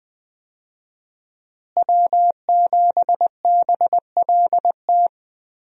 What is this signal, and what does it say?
Morse code, heard as a keyed tone, about 20 wpm, 700 Hz. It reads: W7BLT
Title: USA Call Signs - 5 Characters - Repeated 2x - 20wpm